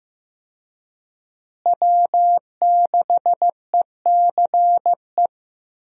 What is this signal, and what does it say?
Morse code, heard as a keyed tone, about 15 wpm, 700 Hz. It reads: W6ECE
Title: USA Call Signs - 5 Characters - Repeated 2x - 15wpm